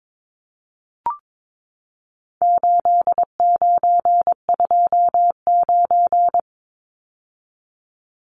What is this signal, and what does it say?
Morse code, heard as a keyed tone, about 22 wpm, 700 Hz. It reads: 8929
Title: Numbers - 4-Digit 22wpm